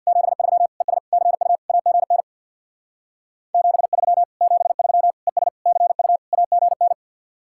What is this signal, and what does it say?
Morse code, heard as a keyed tone, about 64 wpm, 700 Hz. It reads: 73 ES CU AGN 73 73 ES CU AGN